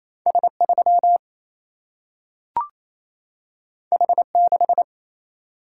Morse code, heard as a keyed tone, about 28 wpm, 700 Hz. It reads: S3 H6
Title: Single-Letter Single-Number 28wpm